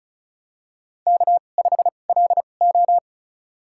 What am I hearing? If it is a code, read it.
Morse code, heard as a keyed tone, about 34 wpm, spaced as for 25 wpm, 700 Hz. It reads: K5LO
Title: USA Call Signs - 4 Characters - 2x Character Spacing - 35wpm